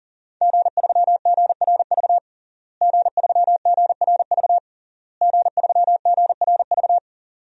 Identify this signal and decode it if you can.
Morse code, heard as a keyed tone, about 37 wpm, 700 Hz. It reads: G3GRV G3GRV G3GRV
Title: DXCC-100 Call Signs - 5 Characters - Repeated 3x - 40wpm